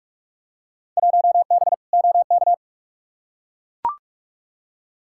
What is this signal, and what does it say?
Morse code, heard as a keyed tone, about 42 wpm, 700 Hz. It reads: 1B OK
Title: Field Day Exchanges Encore - 45wpm